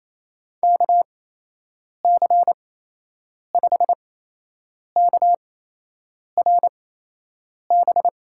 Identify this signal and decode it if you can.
Morse code, heard as a keyed tone, about 28 wpm, spaced as for 8 wpm, 700 Hz. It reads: KC5KRB
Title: USA Call Signs - 6 Characters - 8x Character Spacing - 28wpm